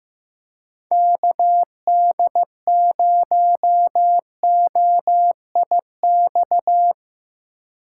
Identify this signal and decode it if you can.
Morse code, heard as a keyed tone, about 15 wpm, 700 Hz. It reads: KD0OIX
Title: K1USN SST QSO Practice - 15wpm